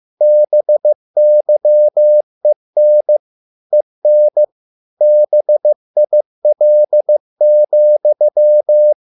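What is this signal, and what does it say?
Morse code, heard as a keyed tone, about 15 wpm, 600 Hz. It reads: BYEN EN BIL,
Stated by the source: Morse code at a slow 15 wpm from CW med Gnister.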